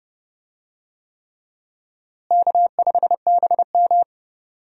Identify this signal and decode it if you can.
Morse code, heard as a keyed tone, about 30 wpm, 700 Hz. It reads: K5BM
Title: USA Call Signs - 4 Characters - Repeated 3x - 30wpm